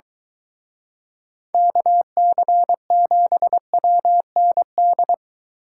Morse code, heard as a keyed tone, about 23 wpm, 700 Hz. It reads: KC7WND